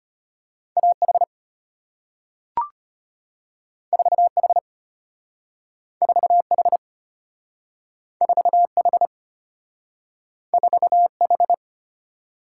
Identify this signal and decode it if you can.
Morse code, heard as a keyed tone, about 34 wpm, 700 Hz. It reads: A H 4 H 4 H 4 H 4 H